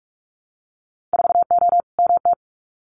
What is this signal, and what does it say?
Morse code, heard as a keyed tone, about 46 wpm, 700 Hz. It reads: 4O MT
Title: Winter Field Day Exchanges Encore 45wpm